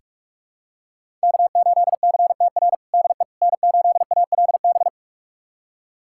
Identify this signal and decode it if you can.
Morse code, heard as a keyed tone, about 42 wpm, 700 Hz. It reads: K8CTR DE N8ALB
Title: USA Callsign DE Callsign - 5 Characters - 45wpm